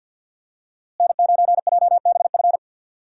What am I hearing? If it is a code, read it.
Morse code, heard as a keyed tone, about 51 wpm, 700 Hz. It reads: N9JBF
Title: USA Call Signs - 5 Characters - Repeated 2x - 50wpm